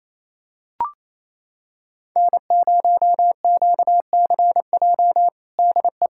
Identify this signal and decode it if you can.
Morse code, heard as a keyed tone, about 28 wpm, 700 Hz. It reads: N0QCJ DE
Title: K1USN SST QSO Practice 28wpm